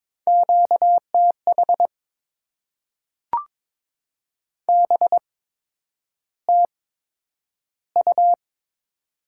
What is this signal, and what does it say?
Morse code, heard as a keyed tone, about 22 wpm, spaced as for 9 wpm, 700 Hz. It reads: QTH BTU
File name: Top 20 QSO Elements - 8x Character Spacing - 22wpm